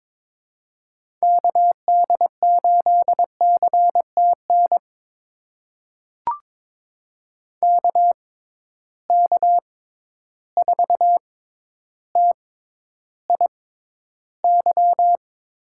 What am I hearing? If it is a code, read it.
Morse code, heard as a keyed tone, about 22 wpm, spaced as for 11 wpm, 700 Hz. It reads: KD8CTN KK4TIY
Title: USA Call Signs - 6 Characters - 6x Character Spacing - 22wpm